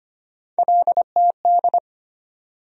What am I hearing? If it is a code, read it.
Morse code, heard as a keyed tone, about 25 wpm, 700 Hz. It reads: LTD
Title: USA Call Sign Postfixes - 3 Letters 25wpm